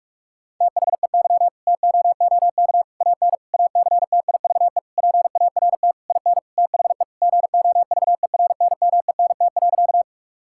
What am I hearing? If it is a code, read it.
Morse code, heard as a keyed tone, about 46 wpm, 700 Hz. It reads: THEY TOOK AN ACTIVE PART IN THE GOVERNMENT.
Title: Sentences from Top 1000 Words - Review Part 6 - 45wpm